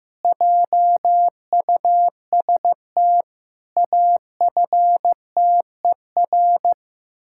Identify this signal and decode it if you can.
Morse code, heard as a keyed tone, about 15 wpm, 700 Hz. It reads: JUST AFTER
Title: Pangram Challenge v2 - 15wpm